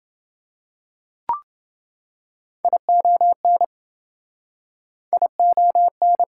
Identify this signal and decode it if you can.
Morse code, heard as a keyed tone, about 27 wpm, 700 Hz. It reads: ION ION